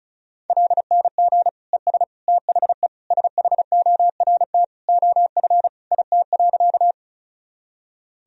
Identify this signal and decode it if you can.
Morse code, heard as a keyed tone, about 36 wpm, 700 Hz. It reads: LNG ES THE SHORT OF IT.
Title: Sentences from Top 400 Words and Abbreviations 35wpm